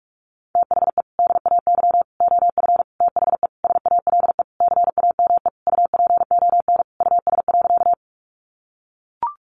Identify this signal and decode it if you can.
Morse code, heard as a keyed tone, about 44 wpm, 700 Hz. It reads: THE DAY OF THE SALE CAME UPON US.
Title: Sentences from Top 1400 Words 45wpm